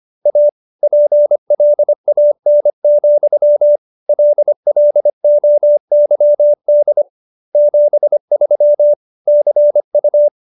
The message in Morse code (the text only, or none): A PLAN, LLOYD 73 CU